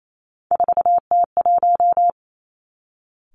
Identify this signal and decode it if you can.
Morse code, heard as a keyed tone, about 28 wpm, 700 Hz. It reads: 4T1
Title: Three Challenging Letter-Number Combinations 28wpm